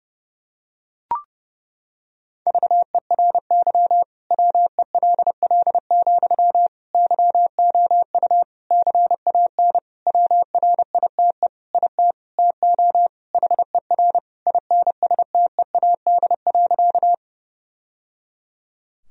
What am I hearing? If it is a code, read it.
Morse code, heard as a keyed tone, about 32 wpm, 700 Hz. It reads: VERY WELL, YOU CAN WRITE IT TO HER INSTEAD.
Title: Sentences from Top 1000 Words - Review Part 4 - 30wpm